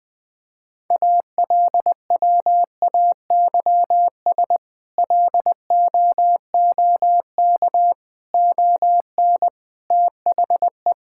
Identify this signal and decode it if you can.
Morse code, heard as a keyed tone, about 20 wpm, 700 Hz. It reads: ALWAYS LOOK ON THE
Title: Sentences with 8-Word Phrases - 20wpm